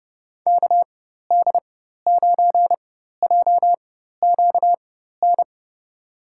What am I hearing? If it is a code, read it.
Morse code, heard as a keyed tone, about 31 wpm, spaced as for 14 wpm, 700 Hz. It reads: KD9JQN
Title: USA Call Signs - 6 Characters - 4x Character Spacing - 30wpm